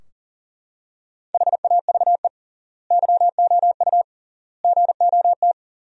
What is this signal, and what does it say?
Morse code, heard as a keyed tone, about 40 wpm, 700 Hz. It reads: HAVE YOU GOT
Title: 3x Word-Spacing - Sentences from Top 300 Words - 40wpm